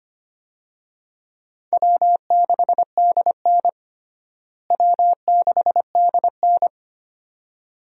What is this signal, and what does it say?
Morse code, heard as a keyed tone, about 25 wpm, 700 Hz. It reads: W6DN W6DN
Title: USA Call Signs - 4 Characters - Repeated 2x - 25wpm